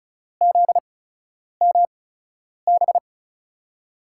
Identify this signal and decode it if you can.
Morse code, heard as a keyed tone, about 34 wpm, spaced as for 10 wpm, 700 Hz. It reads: ZMB
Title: USA Call Signs - 8x Character Spacing - 35wpm